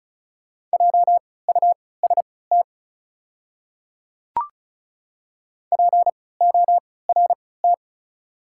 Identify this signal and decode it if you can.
Morse code, heard as a keyed tone, about 36 wpm, spaced as for 20 wpm, 700 Hz. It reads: JUST PORT
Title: 3x Character Spacing - Top 200 Words - 35wpm